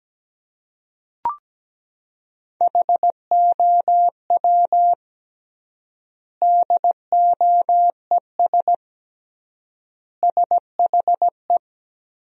Morse code, heard as a keyed tone, about 17 wpm, 700 Hz. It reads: HOW DOES SHE
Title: 3x Word-Spacing - Sentences from Top 200 Words - 17wpm